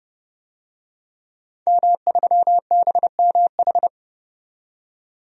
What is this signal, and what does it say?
Morse code, heard as a keyed tone, about 32 wpm, 700 Hz. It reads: M3BMH
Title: DXCC-100 Call Signs - 5 Characters - 30wpm